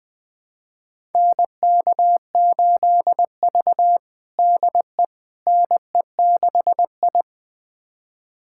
Morse code, heard as a keyed tone, about 20 wpm, 700 Hz. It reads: NK8V DE NE6I
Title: USA Callsign DE Callsign - 4 Characters - 20wpm